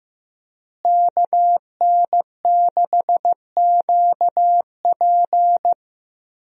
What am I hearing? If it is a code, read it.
Morse code, heard as a keyed tone, about 16 wpm, 700 Hz. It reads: KN6QP
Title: USA Call Signs - 5 Characters - Repeated 3x - 15wpm